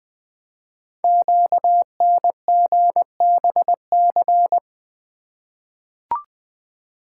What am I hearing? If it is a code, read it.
Morse code, heard as a keyed tone, about 20 wpm, 700 Hz. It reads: QNGBC
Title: Code Groups - 5 Letters - 20wpm